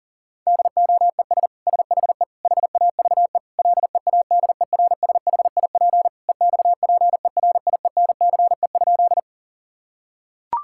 Morse code, heard as a keyed tone, about 40 wpm, 700 Hz. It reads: DOES SHE HAVE LEADERSHIP EXPERIENCE?
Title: Sentences from Top 1600 Words 40wpm